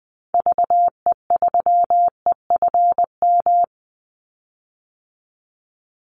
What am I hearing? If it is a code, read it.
Morse code, heard as a keyed tone, about 20 wpm, 700 Hz. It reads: VE3EFM
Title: Call Signs - 20wpm